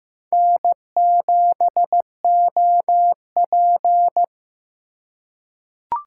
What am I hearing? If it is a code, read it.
Morse code, heard as a keyed tone, about 15 wpm, 700 Hz. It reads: N7OP
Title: USA Call Signs - 4 Characters - 3x Character Spacing - 15wpm